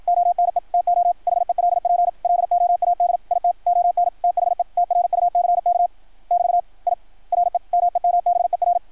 Morse code, heard as a keyed tone, about 53 wpm, 690 Hz, with radio noise. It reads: ONE TO HELP LOAD IT ON THE TRUCK <BT> I REMEMBER
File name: archived / CRSnet